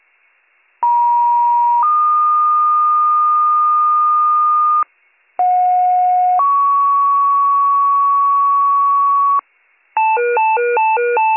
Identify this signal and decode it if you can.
Two-tone page, heard as a one-second tone, then a three-second tone, about 950 then 1220 Hz, about 730 then 1120 Hz. It is heard from 0.8 to 9.4 s.